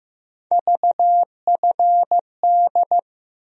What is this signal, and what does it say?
Morse code, heard as a keyed tone, about 16 wpm, 700 Hz. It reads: VFD
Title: USA Call Sign Postfixes - 3 Letters 15wpm